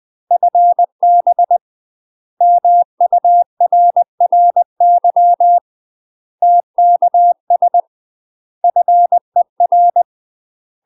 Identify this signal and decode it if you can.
Morse code, heard as a keyed tone, about 20 wpm, 700 Hz. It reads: FB MURRY TKS FER
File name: UK MASTER QSOs TRAINING / 20WPM / UK MASTER QSOs 0005